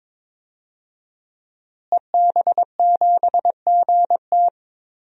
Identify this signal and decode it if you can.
Morse code, heard as a keyed tone, about 22 wpm, 700 Hz. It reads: EB7GT
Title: DXCC-100 Call Signs - 5 Characters - Repeated 2x - 22wpm